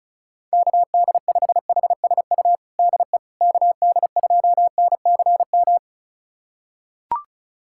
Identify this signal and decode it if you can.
Morse code, heard as a keyed tone, about 34 wpm, 690 Hz. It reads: KD5HSU DE KD2NCM
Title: USA Callsign DE Callsign - 6 Characters - 35wpm